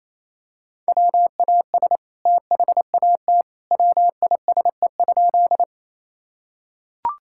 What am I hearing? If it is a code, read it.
Morse code, heard as a keyed tone, about 28 wpm, 700 Hz. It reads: WAS THAT WISE?
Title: Sentences from Top 1600 Words 28wpm